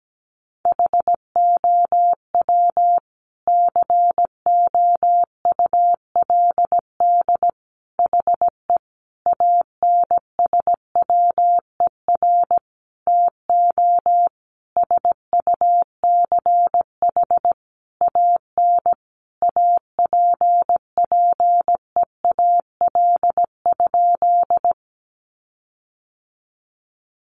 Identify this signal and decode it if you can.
Morse code, heard as a keyed tone, about 17 wpm, 700 Hz. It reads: HOW COULD HE ANSWER TO SUCH AN APPEAL?
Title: Sentences from Top 1300 Words 17wpm